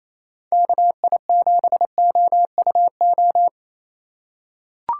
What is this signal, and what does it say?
Morse code, heard as a keyed tone, about 28 wpm, 700 Hz. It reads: KI7OUO
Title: USA Call Signs - 6 Characters - 4x Character Spacing - 28wpm